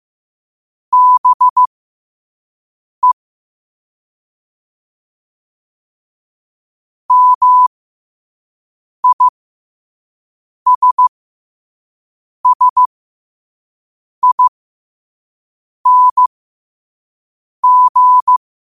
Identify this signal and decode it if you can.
Morse code, heard as a keyed tone, about 15 wpm, spaced as for 5 wpm, 1000 Hz. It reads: BE MISSING
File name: qotdfiles / 5wpm